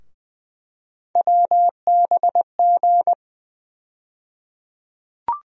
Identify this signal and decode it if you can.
Morse code, heard as a keyed tone, about 20 wpm, 700 Hz. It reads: WBG
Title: USA Call Sign Postfixes - 3 Letters 20wpm